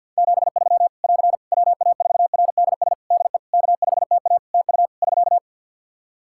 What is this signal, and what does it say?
Morse code, heard as a keyed tone, about 49 wpm, 700 Hz. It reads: 73 <AR> WA4RDS DE K5TA TU <SK>
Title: Practice QSO - Exchange 4 - Ending - 50wpm